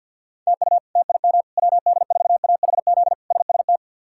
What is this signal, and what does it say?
Morse code, heard as a keyed tone, about 48 wpm, 700 Hz. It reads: TU TIM WD4AHZ SST
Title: K1USN SST QSO Practice - 50wpm